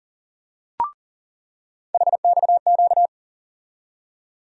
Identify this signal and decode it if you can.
Morse code, heard as a keyed tone, about 40 wpm, 700 Hz. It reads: HXQ
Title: USA Call Sign Postfixes - 3 Letters 40wpm